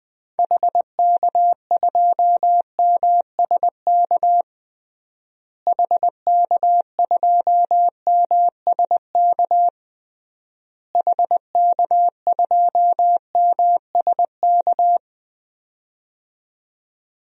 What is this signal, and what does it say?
Morse code, heard as a keyed tone, about 20 wpm, 700 Hz. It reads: HK2MSK HK2MSK HK2MSK
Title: DXCC-100 Call Signs - 6 Characters - Repeated 3x - 20wpm